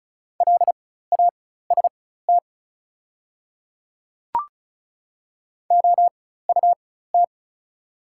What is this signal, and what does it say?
Morse code, heard as a keyed tone, about 34 wpm, spaced as for 16 wpm, 700 Hz. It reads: LAST OUT
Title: 4x Character Spacing - Top 500 Words - 35wpm